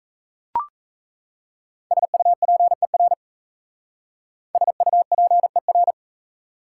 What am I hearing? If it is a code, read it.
Morse code, heard as a keyed tone, about 35 wpm, 700 Hz. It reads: SUPER SUPER